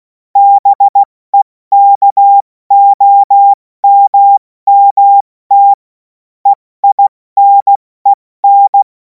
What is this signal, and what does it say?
Morse code, heard as a keyed tone, about 16 wpm, 800 Hz. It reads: BEKOMMT EINEN
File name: Morsekurs / Witze / Witz 17. März 2026 70 F80